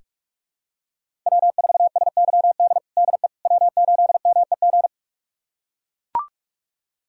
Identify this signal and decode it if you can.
Morse code, heard as a keyed tone, about 42 wpm, 700 Hz. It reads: W4SYD DE W8MEG
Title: USA Callsign DE Callsign - 5 Characters - 45wpm